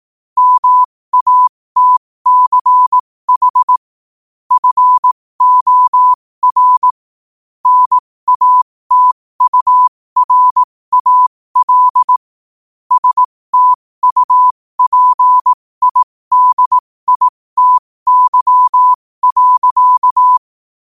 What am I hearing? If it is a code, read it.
Morse code, heard as a keyed tone, about 18 wpm, spaced as for 15 wpm, 1000 Hz. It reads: MATCH FOR NATURAL STUPIDITY.
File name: qotdfiles / 15wpm